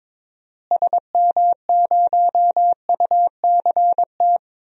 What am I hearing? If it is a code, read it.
Morse code, heard as a keyed tone, about 22 wpm, 700 Hz. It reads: SM0UCT
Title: DXCC-100 Call Signs - 6 Characters - Repeated 3x - 22wpm